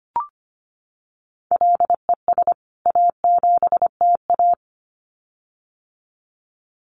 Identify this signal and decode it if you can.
Morse code, heard as a keyed tone, about 25 wpm, 700 Hz. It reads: LES A7TA
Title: CWT Exchange 25wpm